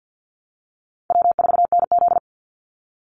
Morse code, heard as a keyed tone, about 50 wpm, 700 Hz. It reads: W4NZ
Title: Easy Call Signs - 50wpm